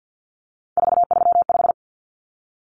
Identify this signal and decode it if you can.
Morse code, heard as a keyed tone, about 50 wpm, 700 Hz. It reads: 435
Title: Common Cut Numbers - 3-Digit 50wpm